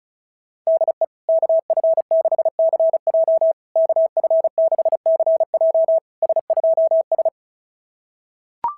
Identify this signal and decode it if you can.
Morse code, heard as a keyed tone, about 36 wpm, 650 Hz. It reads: DE KF6CJ KF6CJ S2S
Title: SOTA QSO Practice 35wpm